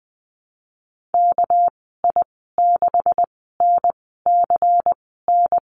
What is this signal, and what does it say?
Morse code, heard as a keyed tone, about 20 wpm, spaced as for 15 wpm, 700 Hz. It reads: KI6NCN
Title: 219 Lesson 35g - Number 0 - Call Signs 20wpm